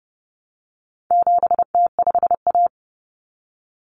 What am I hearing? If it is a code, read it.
Morse code, heard as a keyed tone, about 30 wpm, 700 Hz. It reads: 7T5A
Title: Common Cut Numbers - 4-Digit 30wpm